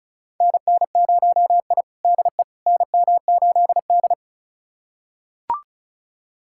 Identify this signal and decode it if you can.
Morse code, heard as a keyed tone, about 36 wpm, 700 Hz. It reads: NN0I DE NM8D
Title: USA Callsign DE Callsign - 4 Characters - 35wpm